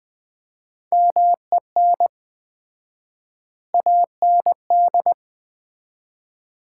Morse code, heard as a keyed tone, about 20 wpm, 700 Hz. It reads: MEN AND
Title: 4x Word-Spacing - Sets of 3 Words - 20wpm